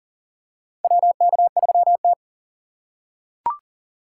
Morse code, heard as a keyed tone, about 40 wpm, 700 Hz. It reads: WK3T
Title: USA Call Signs - 4 Characters - 6x Character Spacing - 40wpm